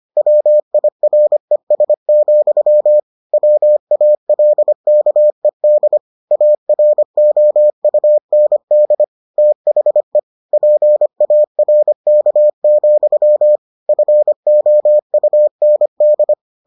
Morse code, heard as a keyed tone, about 25 wpm, 600 Hz. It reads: WIRES, WALKED AROUND THE PARK, FOUND